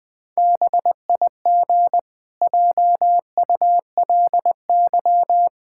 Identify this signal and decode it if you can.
Morse code, heard as a keyed tone, about 20 wpm, 700 Hz. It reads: BIG JULY
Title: Pangram Challenge v2 - 20wpm